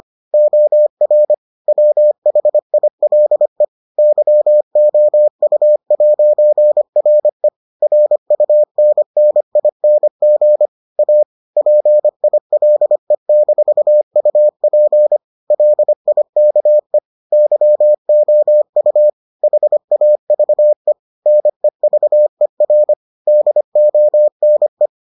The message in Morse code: OR WHILE YOU'RE RUNNING A PILE-UP LIKE YOU HAVE NEVER DONE

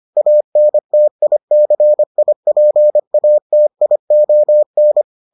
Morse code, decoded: ANTICIPATION

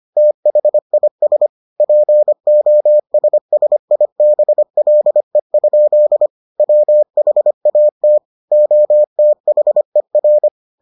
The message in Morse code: THIS POSSIBLE? WHAT OTHER

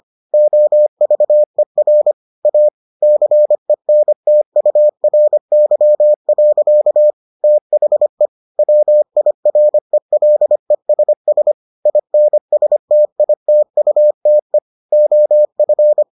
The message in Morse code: OVER A CENTURY. THE WIRELESS INSTITUTE OF

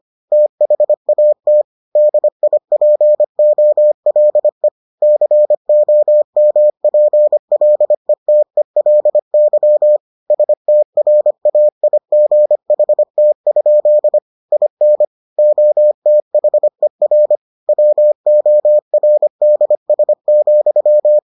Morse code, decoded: THAT DIPOLE COMPLETELY STRAIGHT? IN OTHER WORDS,

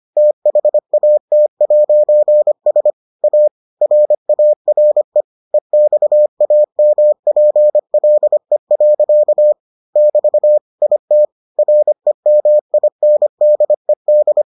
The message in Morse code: THAT'S A RARE EXAMPLE. <BT> IT REMINDED

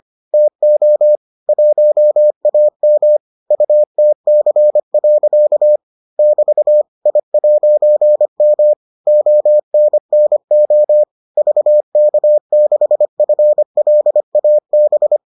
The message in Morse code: TO 1AM UTC. <BT> I'M ONNO VK6FLAB

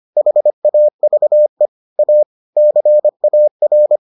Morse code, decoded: HAVE A CAR

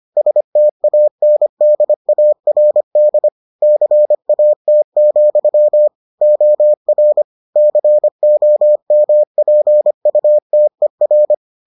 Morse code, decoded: STANDARD CAT, OR COMPUTER

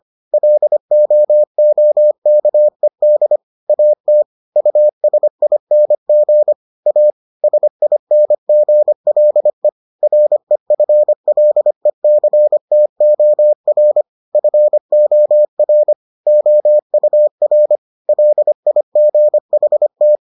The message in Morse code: LOOKED AT USING A SINGLE REFLECTOR FOR OUR LIGHT

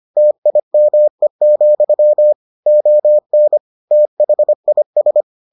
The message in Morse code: TIME, ON THIS

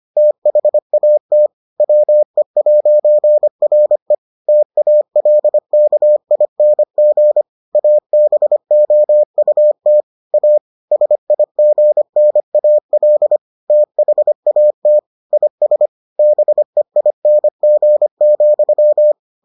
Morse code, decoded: THAT WE'RE TALKING ABOUT A SIGNAL THAT IS BEING,

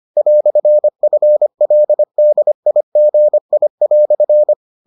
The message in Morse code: "FLDIGI"

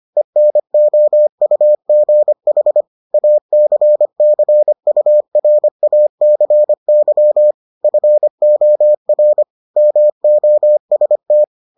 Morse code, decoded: ENOUGH ACCURACY FOR MOST

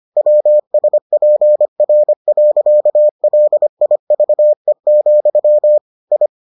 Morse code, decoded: WSPR.LIVE, I

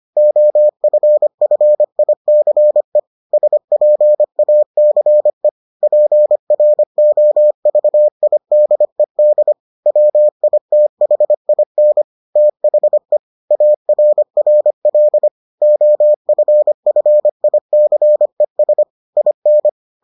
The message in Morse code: OFFICE SPACE PROVIDED WITHIN THE ARRL OFFICES IN

